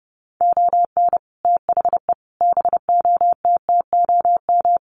Morse code, heard as a keyed tone, about 30 wpm, 700 Hz. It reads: ON THE BOTTOM
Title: Sets of 5 Words Encore2 30wpm